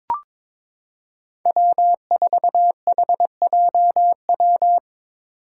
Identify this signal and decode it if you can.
Morse code, heard as a keyed tone, about 22 wpm, 700 Hz. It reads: W4HJW
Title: Five Challenging Letter-Number Combinations 22wpm